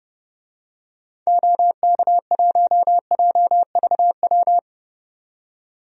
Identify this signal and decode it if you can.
Morse code, heard as a keyed tone, about 29 wpm, 700 Hz. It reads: OK1JVW
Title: DXCC-100 Call Signs - 6 Characters - 30wpm